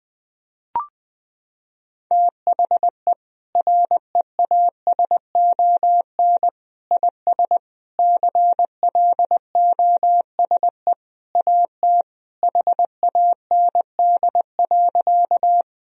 Morse code, heard as a keyed tone, about 20 wpm, 700 Hz. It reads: THE REASON IS CLOSE AT HAND.
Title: Sentences from Top 1000 Words - Review Part 1 - 20wpm